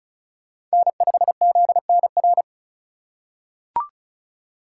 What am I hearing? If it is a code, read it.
Morse code, heard as a keyed tone, about 36 wpm, 700 Hz. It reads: N5ZNR
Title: USA Call Signs - 5 Characters - 4x Character Spacing - 35wpm